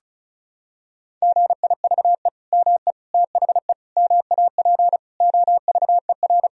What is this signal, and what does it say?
Morse code, heard as a keyed tone, about 36 wpm, 700 Hz. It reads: GIVE ME THE MAP OVER
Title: Sentences from Top 1000 Words - Review Part 2 - 35wpm